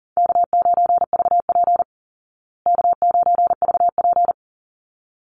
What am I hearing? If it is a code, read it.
Morse code, heard as a keyed tone, about 37 wpm, 700 Hz. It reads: K9VP K9VP
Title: Easy Call Signs Repeated 2x - 40wpm